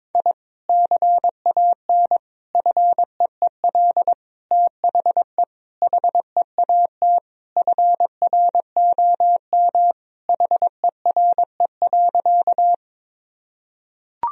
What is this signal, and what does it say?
Morse code, heard as a keyed tone, about 22 wpm, 700 Hz. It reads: I CAN FEEL THE HEAT FROM HERE.
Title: Sentences from Top 1000 Words - Review Part 3 - 22wpm